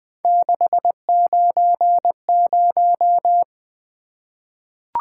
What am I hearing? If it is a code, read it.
Morse code, heard as a keyed tone, about 20 wpm, 700 Hz. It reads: 690
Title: Three Letter-Number Combinations - 20wpm